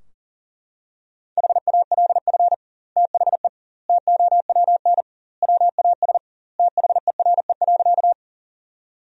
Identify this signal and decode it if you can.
Morse code, heard as a keyed tone, about 40 wpm, 700 Hz. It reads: HALF THE TOWN WAS THERE.
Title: 2x Word-Spacing - Sentences from Top 500 Words - 40wpm